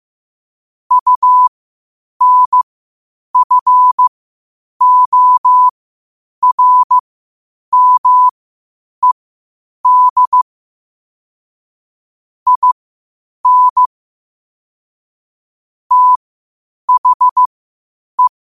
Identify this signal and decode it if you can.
Morse code, heard as a keyed tone, about 15 wpm, spaced as for 8 wpm, 1000 Hz. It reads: UNFORMED IN THE
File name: qotdfiles / 8wpm